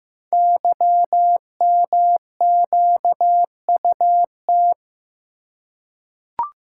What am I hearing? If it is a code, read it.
Morse code, heard as a keyed tone, about 15 wpm, 700 Hz. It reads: YMQUT
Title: Code Groups - 5 Letters - 15wpm